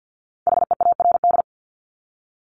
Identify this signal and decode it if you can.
Morse code, heard as a keyed tone, about 50 wpm, 700 Hz. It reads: HEARD
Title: Top 500 Words Repeated 50wpm